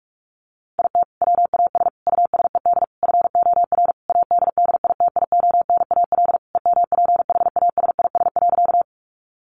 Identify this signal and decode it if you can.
Morse code, heard as a keyed tone, about 44 wpm, 700 Hz. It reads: IT WAS USED FOR ADDITIONAL EMPHASIS.
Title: Sentences from Top 1400 Words 45wpm